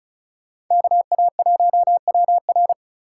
Morse code, heard as a keyed tone, about 45 wpm, 700 Hz. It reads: KA1WR